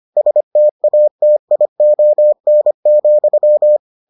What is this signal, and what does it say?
Morse code, heard as a keyed tone, about 25 wpm, 600 Hz. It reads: STATION,